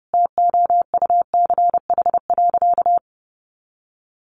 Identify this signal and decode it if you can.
Morse code, heard as a keyed tone, about 30 wpm, 700 Hz. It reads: TOUCH.
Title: Sentences from Top 1500 Words 30wpm